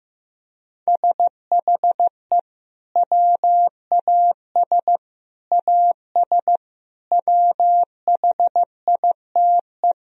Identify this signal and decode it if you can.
Morse code, heard as a keyed tone, about 16 wpm, 700 Hz. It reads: SHE WAS AS WHITE
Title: Sentences from Top 1000 Words - Review Part 1 - 15wpm